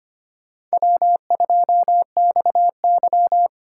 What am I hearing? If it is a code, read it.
Morse code, heard as a keyed tone, about 31 wpm, 700 Hz. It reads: W2XY
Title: Speed Racing - USA 1x2 Call Signs 25wpm